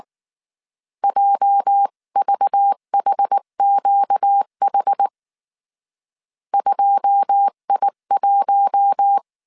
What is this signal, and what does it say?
Morse code, heard as a keyed tone, about 19 wpm, 800 Hz. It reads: JVHQH 2I1
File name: CW / DET / 1 9 1500C S C 18wpm 2539